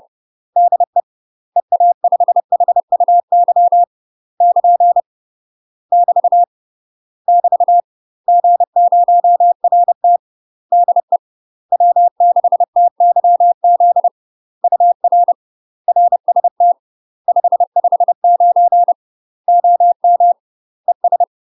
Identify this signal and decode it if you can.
Morse code, heard as a keyed tone, about 30 wpm, 700 Hz. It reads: DE EA5HUY <KN> <BT> <BT> G0RT DE W6TYZ UR RST 559 OM ES
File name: UK MASTER QSOs TRAINING / 30WPM / UK MASTER QSOs 0046